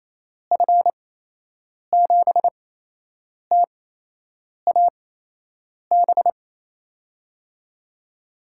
Morse code, heard as a keyed tone, about 28 wpm, spaced as for 8 wpm, 700 Hz. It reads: F7TAB
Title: USA Call Signs - 6 Characters - 8x Character Spacing - 28wpm